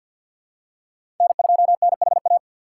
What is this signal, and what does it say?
Morse code, heard as a keyed tone, about 48 wpm, 700 Hz. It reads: N2NHA
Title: USA Call Signs - 5 Characters - 6x Character Spacing - 50wpm